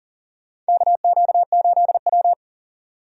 Morse code, heard as a keyed tone, about 40 wpm, 700 Hz. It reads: KQ8W
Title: USA Call Signs - 4 Characters - 40wpm